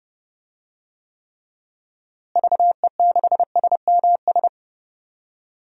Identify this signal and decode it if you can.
Morse code, heard as a keyed tone, about 31 wpm, 700 Hz. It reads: VE6SMS
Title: DXCC-100 Call Signs - 6 Characters - Repeated 2x - 30wpm